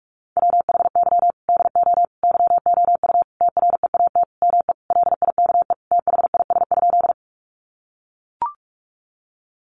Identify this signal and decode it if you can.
Morse code, heard as a keyed tone, about 44 wpm, 700 Hz. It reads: WHY DO YOU TREAT ME LIKE THIS?
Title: Sentences from Top 1300 Words 45wpm